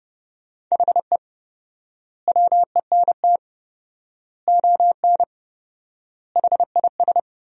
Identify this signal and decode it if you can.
Morse code, heard as a keyed tone, about 30 wpm, 700 Hz. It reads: HE WENT ON HIS
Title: 4x Word-Spacing - Sentences from Top 200 Words - 30wpm